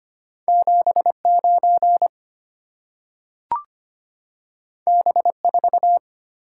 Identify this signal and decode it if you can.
Morse code, heard as a keyed tone, about 25 wpm, 700 Hz. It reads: 79 B4
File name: Two Letter-Number Combinations - 25wpm